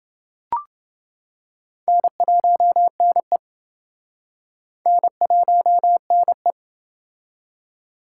Morse code, heard as a keyed tone, about 29 wpm, 700 Hz. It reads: N1NE N1NE